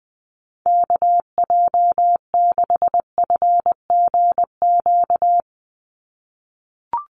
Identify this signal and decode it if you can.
Morse code, heard as a keyed tone, about 20 wpm, 700 Hz. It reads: KJ6FGQ
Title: USA Call Signs 20wpm